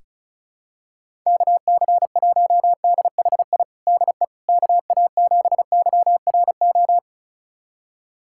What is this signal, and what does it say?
Morse code, heard as a keyed tone, about 34 wpm, 700 Hz. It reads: KC1DHI DE KA7YRO
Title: USA Callsign DE Callsign - 6 Characters - 35wpm